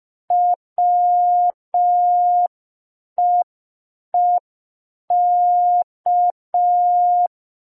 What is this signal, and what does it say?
Morse code, heard as a keyed tone, about 5 wpm, 710 Hz. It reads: WEEK